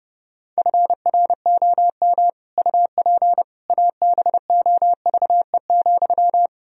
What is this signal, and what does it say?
Morse code, heard as a keyed tone, about 30 wpm, 700 Hz. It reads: FROM UP ABOVE,
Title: Sentences from Top 1000 Words - Review Part 2 - 30wpm